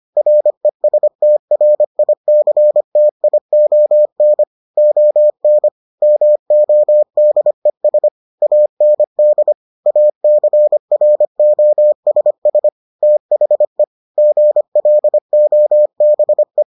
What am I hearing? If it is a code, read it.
Morse code, heard as a keyed tone, about 25 wpm, 600 Hz. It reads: RESTRICTION ON MODES AND ACROSS THE GLOBE